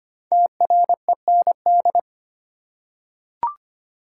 Morse code, heard as a keyed tone, about 25 wpm, 700 Hz. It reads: TREND
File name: 5 Letter Words - First 3 Letters Given - 25wpm